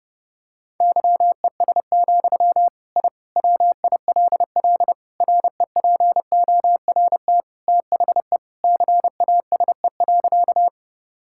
Morse code, heard as a keyed tone, about 30 wpm, 700 Hz. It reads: YES, I WILL REPORT THE CASE.
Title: Sentences from Top 1000 Words - Review Part 6 - 30wpm